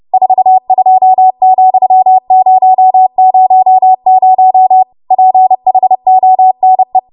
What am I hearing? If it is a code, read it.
Morse code, heard as a keyed tone, about 30 wpm, 750 Hz. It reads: 42,000 PHONE
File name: Archive / 30 WPM / 30WPM